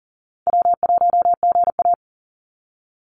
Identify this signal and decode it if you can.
Morse code, heard as a keyed tone, about 40 wpm, 700 Hz. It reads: W1GA
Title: USA Call Signs 40wpm